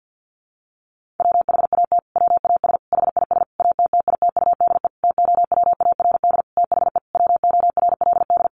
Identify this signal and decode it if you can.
Morse code, heard as a keyed tone, about 50 wpm, 700 Hz. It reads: WHAT WAS HIS ATTITUDE TOWARD THE WORLD
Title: Sentences from Top 1300 Words 50wpm